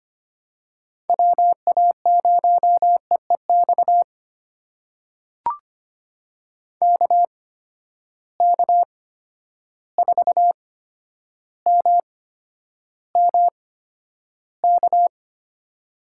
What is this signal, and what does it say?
Morse code, heard as a keyed tone, about 25 wpm, spaced as for 11 wpm, 700 Hz. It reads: WA0EEX KK4MMK
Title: USA Call Signs - 6 Characters - 8x Character Spacing - 25wpm